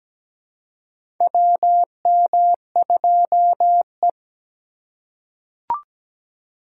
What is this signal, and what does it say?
Morse code, heard as a keyed tone, about 17 wpm, 700 Hz. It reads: WM2E
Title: USA Call Signs - 4 Characters - 3x Character Spacing - 17wpm